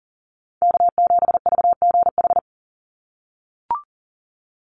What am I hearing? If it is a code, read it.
Morse code, heard as a keyed tone, about 40 wpm, 700 Hz. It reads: K7VGH
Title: USA Call Signs 40wpm